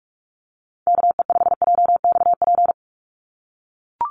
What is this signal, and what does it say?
Morse code, heard as a keyed tone, about 44 wpm, 700 Hz. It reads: KE5JXP
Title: USA Call Signs 45wpm